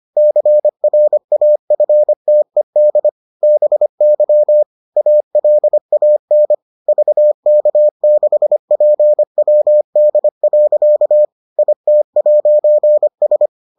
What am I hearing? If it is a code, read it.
Morse code, heard as a keyed tone, about 25 wpm, 600 Hz. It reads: CRAFTED BY ALAN VK6PWD. IT'S